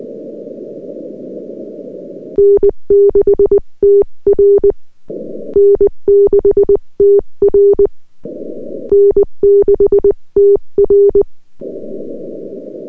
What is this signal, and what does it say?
Morse code, heard as a keyed tone, about 20 wpm, 400 Hz. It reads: N6TR N6TR N6TR
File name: n6tr / VP8-LZ1UQ-N6TR